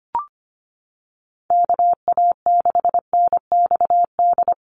Morse code, heard as a keyed tone, about 25 wpm, 700 Hz. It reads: KA6NXD
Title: USA Call Signs 25wpm